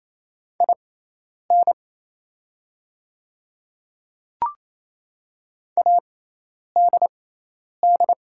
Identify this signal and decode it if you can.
Morse code, heard as a keyed tone, about 28 wpm, spaced as for 10 wpm, 690 Hz. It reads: IN ADD
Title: 6x Character Spacing - Top 500 Words - 28wpm